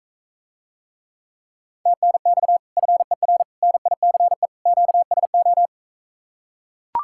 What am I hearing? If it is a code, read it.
Morse code, heard as a keyed tone, about 44 wpm, 700 Hz. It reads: TNX FER NICE QSO